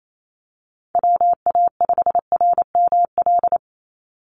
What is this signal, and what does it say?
Morse code, heard as a keyed tone, about 28 wpm, 700 Hz. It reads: WA5RML
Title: Call Signs - 28wpm